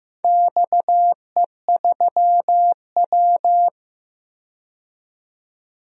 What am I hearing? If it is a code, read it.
Morse code, heard as a keyed tone, about 15 wpm, 700 Hz. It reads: XE3W
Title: DXCC-100 Call Signs - 4 Characters - 15wpm